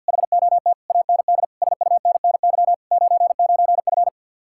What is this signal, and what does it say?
Morse code, heard as a keyed tone, about 60 wpm, 700 Hz. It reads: HOT AND SUNNY 99F
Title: Speed-Racing Common QSO Phrases - 50wpm